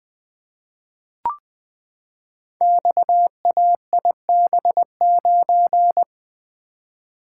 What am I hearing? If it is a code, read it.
Morse code, heard as a keyed tone, about 20 wpm, 700 Hz. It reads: XAIB9
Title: Code Groups - 5 Letter:Numbers - 20wpm